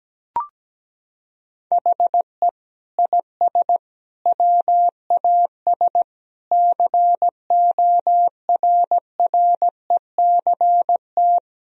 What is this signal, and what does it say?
Morse code, heard as a keyed tone, about 17 wpm, 700 Hz. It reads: HE IS WAS CORRECT
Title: Sentences from Top 1000 Words - Review Part 2 - 17wpm